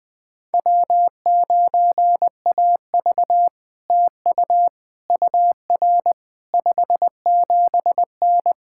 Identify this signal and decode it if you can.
Morse code, heard as a keyed tone, about 20 wpm, 700 Hz. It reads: W9AV TU UR 57N
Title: POTA QSO Practice - 20wpm